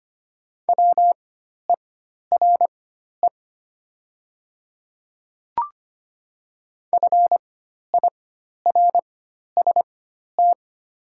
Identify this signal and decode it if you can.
Morse code, heard as a keyed tone, about 25 wpm, spaced as for 12 wpm, 700 Hz. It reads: WERE FIRST
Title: 4x Character Spacing - Top 500 Words - 25wpm